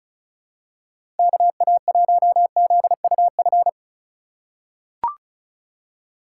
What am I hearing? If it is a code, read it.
Morse code, heard as a keyed tone, about 36 wpm, 700 Hz. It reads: KA1ZUF
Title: USA Call Signs - 6 Characters - 3x Character Spacing - 35wpm